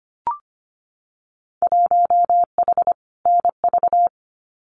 Morse code, heard as a keyed tone, about 25 wpm, 700 Hz. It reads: 1H NV
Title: Winter Field Day Exchanges 25wpm